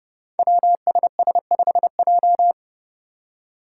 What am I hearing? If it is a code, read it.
Morse code, heard as a keyed tone, about 30 wpm, 700 Hz. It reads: WSS5J
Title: Five Challenging Letter-Number Combinations 30wpm